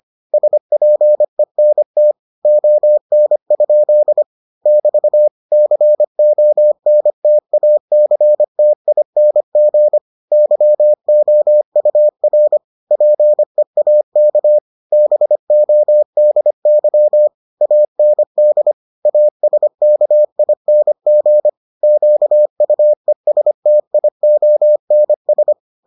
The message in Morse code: SPENT ON? <BT> CONTACTING YOUR PEAK BODY AND ASKING QUESTIONS

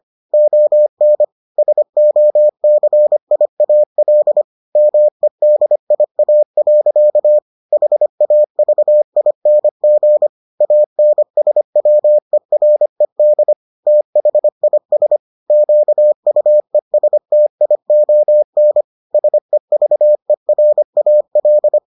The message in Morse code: ON SOCIAL MEDIA. HAVING ANSWERED THIS QUESTION SEVERAL

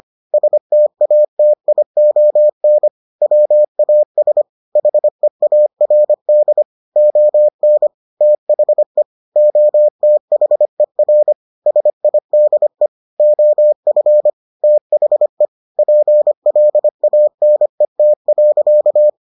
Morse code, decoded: STATION WAS HEARD ON THE OTHER SIDE OF THE PLANET.

